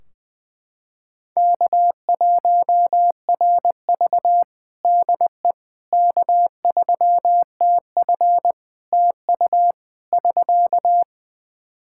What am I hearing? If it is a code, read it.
Morse code, heard as a keyed tone, about 20 wpm, 700 Hz. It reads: K1RV DE K3TF TU <SK>
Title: Practice QSO - Exchange 4 - Ending - 20wpm